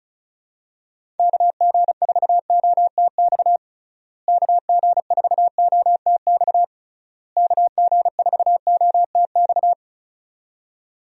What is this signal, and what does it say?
Morse code, heard as a keyed tone, about 34 wpm, 700 Hz. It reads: KG4OTX KG4OTX KG4OTX
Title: USA Call Signs - 6 Characters - Repeated 3x - 35wpm